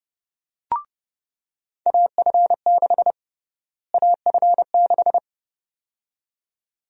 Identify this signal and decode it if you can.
Morse code, heard as a keyed tone, about 30 wpm, 700 Hz. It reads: AF6 AF6
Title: USA Call Sign Prefixes - Repeated 2x - 30wpm